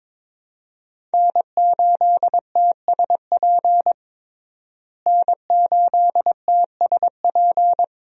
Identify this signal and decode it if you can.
Morse code, heard as a keyed tone, about 22 wpm, 700 Hz. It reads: N8TSP N8TSP
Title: USA Call Signs - 5 Characters - Repeated 2x - 22wpm